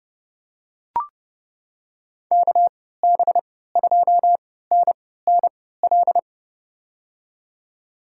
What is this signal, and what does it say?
Morse code, heard as a keyed tone, about 30 wpm, spaced as for 17 wpm, 700 Hz. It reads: KB2NNL
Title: USA Call Signs - 6 Characters - 3x Character Spacing - 30wpm